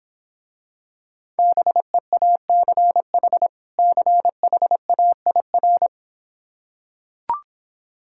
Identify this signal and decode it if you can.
Morse code, heard as a keyed tone, about 26 wpm, 700 Hz. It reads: BEACH CHAIR